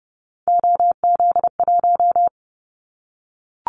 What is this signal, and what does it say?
Morse code, heard as a keyed tone, about 30 wpm, 700 Hz. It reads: OZ1
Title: Callsign Prefixes for Top 100 DXCC - 30wpm